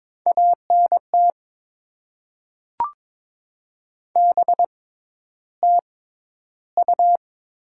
Morse code, heard as a keyed tone, about 22 wpm, spaced as for 11 wpm, 690 Hz. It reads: ANT BTU
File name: Top 20 QSO Elements - 6x Character Spacing - 22wpm